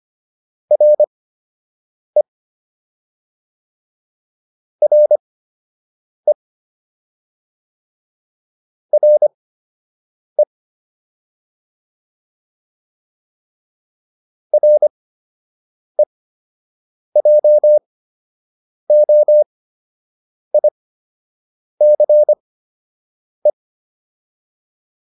Morse code, heard as a keyed tone, about 25 wpm, spaced as for 7 wpm, 600 Hz. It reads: RE RE RE REJOICE